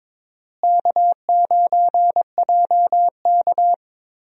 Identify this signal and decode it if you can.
Morse code, heard as a keyed tone, about 23 wpm, 700 Hz. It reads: K9JK
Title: USA Call Signs - 4 Characters - Repeated 2x - 22wpm